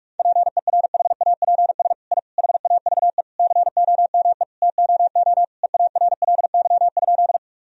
Morse code, heard as a keyed tone, about 41 wpm, 700 Hz. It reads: PERHAPS I HAVE COME TOO EARLY?
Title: Sentences from Top 1000 Words - Review Part 2 - 45wpm